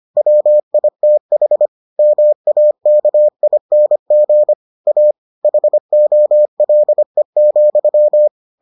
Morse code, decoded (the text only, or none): WITH MAKING A HOLE,